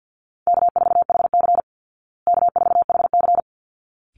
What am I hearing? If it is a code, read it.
Morse code, heard as a keyed tone, about 50 wpm, 700 Hz. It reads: K4HC K4HC
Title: Easy Call Signs Repeated 2x - 50wpm